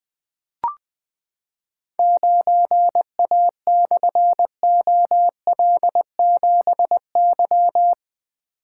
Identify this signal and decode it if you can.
Morse code, heard as a keyed tone, about 20 wpm, 700 Hz. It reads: 9A/OL7Y
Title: Difficult Call Signs 20wpm